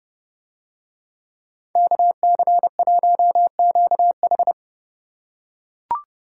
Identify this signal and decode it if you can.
Morse code, heard as a keyed tone, about 30 wpm, 700 Hz. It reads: KC1QH